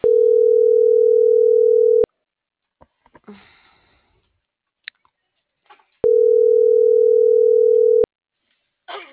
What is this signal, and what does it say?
Ringback tone (North America), 440 + 480 Hz.